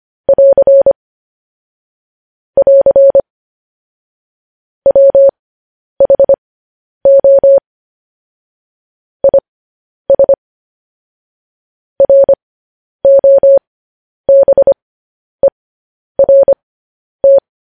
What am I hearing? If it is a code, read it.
Morse code, heard as a keyed tone, about 25 wpm, spaced as for 10 wpm, 550 Hz. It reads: <AR> <AR> WHO IS ROBERT
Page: News Headlines in Morse Code at 10 WPM